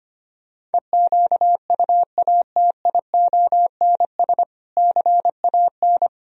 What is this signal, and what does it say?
Morse code, heard as a keyed tone, about 24 wpm, 700 Hz. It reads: EQUATIONS CAN
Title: Speed-Racing - Sets of 5 Words 25wpm